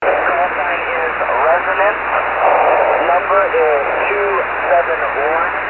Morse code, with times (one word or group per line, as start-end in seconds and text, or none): none